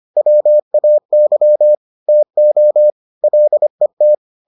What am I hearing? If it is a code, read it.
Morse code, heard as a keyed tone, about 25 wpm, 600 Hz. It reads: WAY TO LET